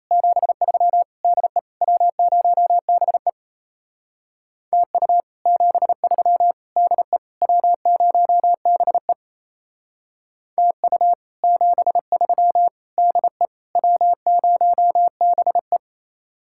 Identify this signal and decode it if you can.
Morse code, heard as a keyed tone, about 36 wpm, 700 Hz. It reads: 73 DE W0BE TU 73 DE W0BE TU 73 DE W0BE